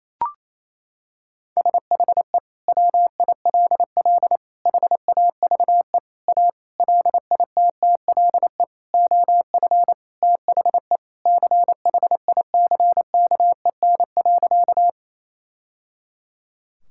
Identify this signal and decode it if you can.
Morse code, heard as a keyed tone, about 27 wpm, 700 Hz. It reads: SHE WILL HAVE A LITTLE OF THE CHICKEN.
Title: Sentences from Top 1000 Words - Review Part 5 - 28wpm